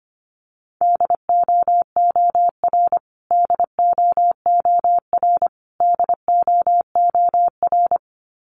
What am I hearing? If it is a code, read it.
Morse code, heard as a keyed tone, about 25 wpm, 700 Hz. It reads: DOOR DOOR DOOR
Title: Top 500 Words Repeated 25wpm